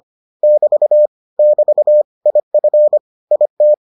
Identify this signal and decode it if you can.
Morse code, heard as a keyed tone, about 25 wpm, 600 Hz. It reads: <BT> <BT> IF IT